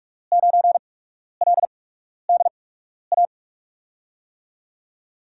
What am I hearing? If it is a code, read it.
Morse code, heard as a keyed tone, about 44 wpm, spaced as for 12 wpm, 700 Hz. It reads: 9LDA
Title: USA Call Signs - 6 Characters - 8x Character Spacing - 45wpm